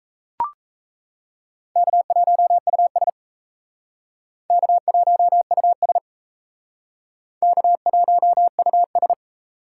Morse code, heard as a keyed tone, about 38 wpm, 700 Hz. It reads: K1US K1US K1US